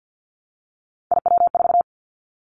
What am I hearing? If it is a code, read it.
Morse code, heard as a keyed tone, about 48 wpm, 700 Hz. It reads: IW4
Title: Callsign Prefixes for Top 25 DXCC - Repeated 3x - 50wpm